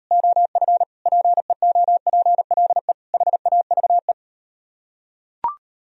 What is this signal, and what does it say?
Morse code, heard as a keyed tone, about 38 wpm, 700 Hz. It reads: OF PEOPLE HAVE